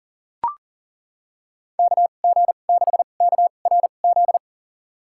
Morse code, heard as a keyed tone, about 40 wpm, spaced as for 29 wpm, 700 Hz. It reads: KG6KRZ
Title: USA Call Signs - 6 Characters - 2x Character Spacing - 40wpm